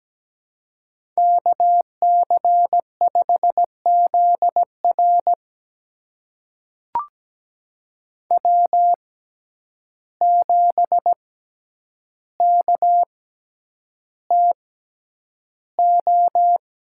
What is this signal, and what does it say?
Morse code, heard as a keyed tone, about 17 wpm, spaced as for 9 wpm, 700 Hz. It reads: KC5ZR W7KTO
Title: USA Call Signs - 6x Character Spacing - 17wpm